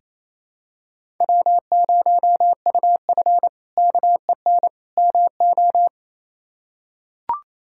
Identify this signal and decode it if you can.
Morse code, heard as a keyed tone, about 28 wpm, 700 Hz. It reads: W0UF KEN MO
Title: K1USN SST QSO Practice - 28wpm